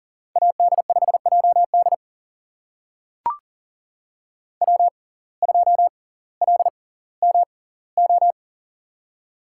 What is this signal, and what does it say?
Morse code, heard as a keyed tone, about 39 wpm, spaced as for 20 wpm, 690 Hz. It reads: AD5JD W2LMO
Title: USA Call Signs - 5 Characters - 6x Character Spacing - 40wpm